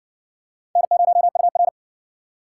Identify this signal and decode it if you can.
Morse code, heard as a keyed tone, about 60 wpm, 700 Hz. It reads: N0UR